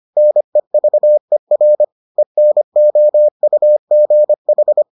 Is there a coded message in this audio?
Yes: Morse code from 0.2 to 4.8 s, heard as a keyed tone, about 25 wpm, 600 Hz. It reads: NEVER ENOUGH